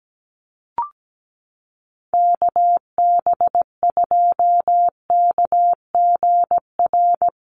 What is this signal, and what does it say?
Morse code, heard as a keyed tone, about 17 wpm, 700 Hz. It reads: KB2KGR
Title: USA Call Signs 17wpm